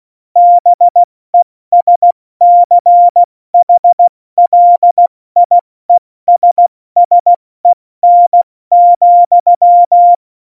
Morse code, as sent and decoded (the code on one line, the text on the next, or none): -... ... -.-. .... .-.. .. ... ... -. --..--
BESCHLIESSEN,